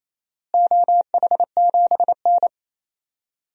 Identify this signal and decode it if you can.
Morse code, heard as a keyed tone, about 27 wpm, 700 Hz. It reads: OH7N
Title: DXCC-100 Call Signs - 4 Characters - Repeated 3x - 28wpm